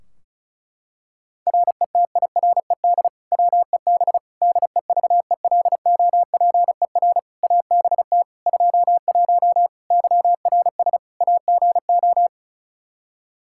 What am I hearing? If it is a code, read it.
Morse code, heard as a keyed tone, about 34 wpm, 700 Hz. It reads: RETIRED WEB DEVELOPER ABT 21 YRS AGO
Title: Practice QSO - Exchange 3 - More Info - 35wpm